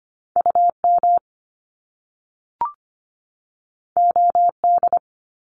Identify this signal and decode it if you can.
Morse code, heard as a keyed tone, about 25 wpm, 700 Hz. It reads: UM OD
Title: Top Two Letter Combinations in Words 25wpm